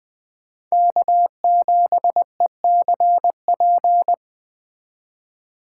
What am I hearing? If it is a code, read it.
Morse code, heard as a keyed tone, about 20 wpm, 700 Hz. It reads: K7ECP